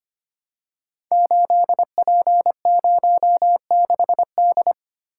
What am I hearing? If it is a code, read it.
Morse code, heard as a keyed tone, about 25 wpm, 700 Hz. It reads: 8P06D
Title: Code Groups - 5 Letter:Numbers - 25wpm